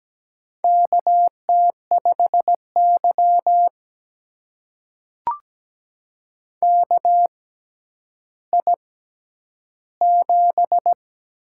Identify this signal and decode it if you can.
Morse code, heard as a keyed tone, about 17 wpm, spaced as for 10 wpm, 700 Hz. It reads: KT5Y KI7
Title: USA Call Signs - 6x Character Spacing - 17wpm